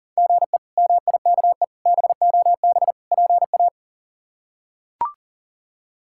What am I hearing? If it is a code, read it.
Morse code, heard as a keyed tone, about 40 wpm, 700 Hz. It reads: GE MIKE BOB PA
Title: K1USN SST QSO Practice - 40wpm